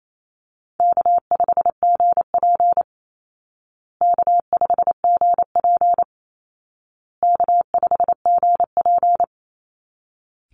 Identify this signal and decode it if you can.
Morse code, heard as a keyed tone, about 28 wpm, 700 Hz. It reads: K5GP K5GP K5GP
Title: Call Signs Repeated 2x - 28wpm